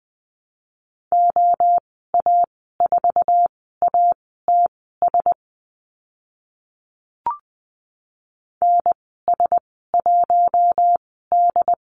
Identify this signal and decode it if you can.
Morse code, heard as a keyed tone, about 20 wpm, spaced as for 14 wpm, 700 Hz. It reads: OA4ATS NS1D
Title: 082 Lesson 14g - Number 2 - Call Signs 20wpm